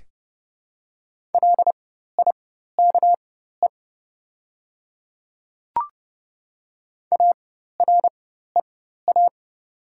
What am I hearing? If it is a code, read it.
Morse code, heard as a keyed tone, about 30 wpm, spaced as for 14 wpm, 700 Hz. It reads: LIKE AREA
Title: 4x Character Spacing - Top 500 Words - 30wpm